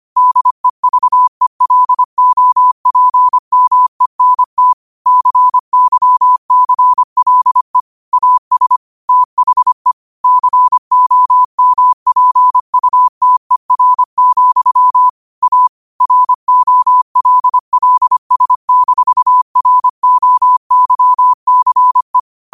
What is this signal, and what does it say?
Morse code, heard as a keyed tone, about 25 wpm, 1000 Hz. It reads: DEVELOPMENT CYCLE AS THE COMPUTER, A ROLLS-ROYCE
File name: qotdfiles / 25wpm